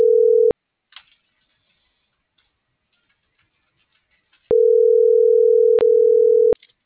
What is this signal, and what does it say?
Ringback tone (North America), 440 + 480 Hz.